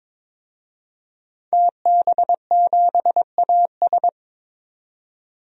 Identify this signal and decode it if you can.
Morse code, heard as a keyed tone, about 22 wpm, 700 Hz. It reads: TB7AS
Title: DXCC-100 Call Signs - 5 Characters - 22wpm